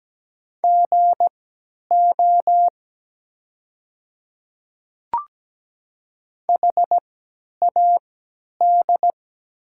Morse code, heard as a keyed tone, about 17 wpm, spaced as for 10 wpm, 700 Hz. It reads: GO HAD
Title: 3x Character Spacing - Top 100 Words - 17wpm